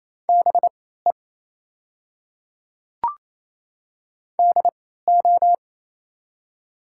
Morse code, heard as a keyed tone, about 27 wpm, spaced as for 16 wpm, 700 Hz. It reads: BE DO
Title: 3x Character Spacing - Top 200 Words - 28wpm